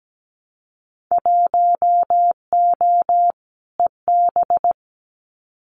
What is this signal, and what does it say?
Morse code, heard as a keyed tone, about 17 wpm, 700 Hz. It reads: 1O EB
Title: Winter Field Day Exchanges 17wpm